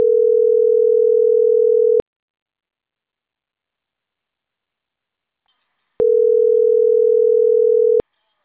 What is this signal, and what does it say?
Ringback tone (North America), 440 + 480 Hz.